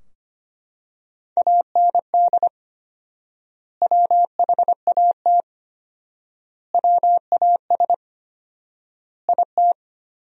Morse code, heard as a keyed tone, about 25 wpm, 700 Hz. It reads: AND WHAT WAS IT
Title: 4x Word-Spacing - Sentences from Top 100 Words - 25wpm